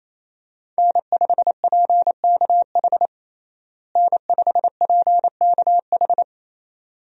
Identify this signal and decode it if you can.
Morse code, heard as a keyed tone, about 28 wpm, 700 Hz. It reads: N5PKH N5PKH
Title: USA Call Signs - 5 Characters - Repeated 2x - 28wpm